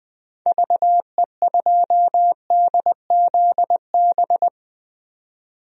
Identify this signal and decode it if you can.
Morse code, heard as a keyed tone, about 20 wpm, 700 Hz. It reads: VE2DZB
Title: DXCC-100 Call Signs - 6 Characters - 20wpm